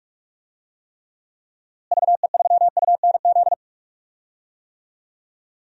Morse code, heard as a keyed tone, about 47 wpm, 700 Hz. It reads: VE3UNZ
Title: DXCC-100 Call Signs - 6 Characters - Repeated 2x - 45wpm